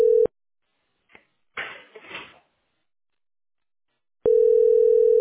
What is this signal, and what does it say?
Ringback tone (North America), 440 + 480 Hz.